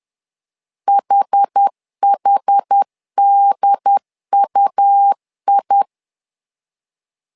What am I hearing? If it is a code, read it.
Morse code, heard as a keyed tone, about 10 wpm, 800 Hz. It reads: HHDUI
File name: LEZIONI / 2 UHDN / 2 UHD 10wpm 1826